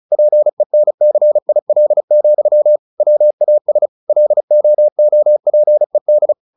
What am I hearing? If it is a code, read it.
Morse code, heard as a keyed tone, about 34 wpm, 600 Hz. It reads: PENCIL, WAS LOOPED
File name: the old man and the sea / the old man and the sea 1 35wpm